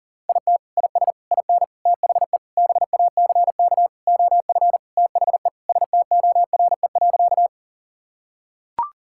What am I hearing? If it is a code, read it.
Morse code, heard as a keyed tone, about 40 wpm, 700 Hz. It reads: IT IS IN THE BACK OF THE STORE.
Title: Sentences from Top 1000 Words - Review Part 5 - 40wpm